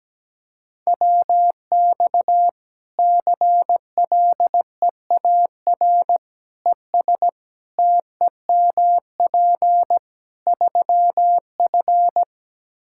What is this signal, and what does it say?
Morse code, heard as a keyed tone, about 17 wpm, 700 Hz. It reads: WX CLEAR ES TEMP 3F
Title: Practice QSO - Exchange 2 - Info - 17wpm